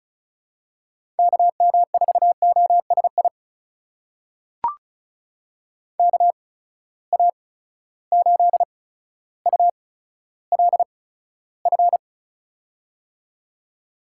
Morse code, heard as a keyed tone, about 36 wpm, spaced as for 15 wpm, 700 Hz. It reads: KM4OSI KA8ULF
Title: USA Call Signs - 6 Characters - 8x Character Spacing - 35wpm